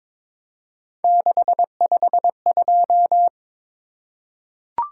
Morse code, heard as a keyed tone, about 23 wpm, 700 Hz. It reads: 652
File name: practice / 22wpm / Three Letter-Number Combinations - 22wpm